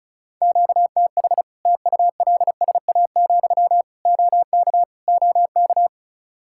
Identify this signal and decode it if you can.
Morse code, heard as a keyed tone, about 34 wpm, 700 Hz. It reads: QTH TULSA, OK OK
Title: Practice QSO - Exchange 1 - Beginning - 35wpm